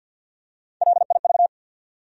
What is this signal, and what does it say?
Morse code, heard as a keyed tone, about 52 wpm, 700 Hz. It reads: LIV
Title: USA Common Names - Max 5 Letters - 50wpm